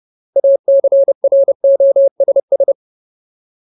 Morse code, heard as a keyed tone, about 30 wpm, 550 Hz. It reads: ACROSS
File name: Ultimate Word List / 30WPM Ultimate